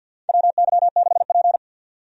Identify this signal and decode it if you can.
Morse code, heard as a keyed tone, about 52 wpm, 700 Hz. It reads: WY6P
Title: USA Call Signs - 4 Characters - 50wpm